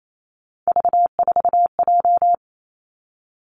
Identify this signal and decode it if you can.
Morse code, heard as a keyed tone, about 28 wpm, 700 Hz. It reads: V4J
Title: Three Challenging Letter-Number Combinations 28wpm